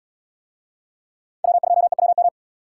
Morse code, heard as a keyed tone, about 77 wpm, 700 Hz. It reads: W2EWM